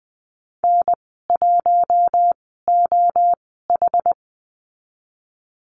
Morse code, heard as a keyed tone, about 20 wpm, spaced as for 14 wpm, 700 Hz. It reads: N1OH
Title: 062 Lesson 11g - Letter h - Call Signs 20wpm